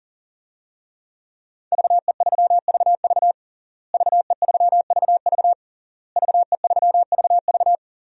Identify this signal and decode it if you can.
Morse code, heard as a keyed tone, about 40 wpm, 700 Hz. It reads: VE3VV VE3VV VE3VV
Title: DXCC-100 Call Signs - 5 Characters - Repeated 3x - 40wpm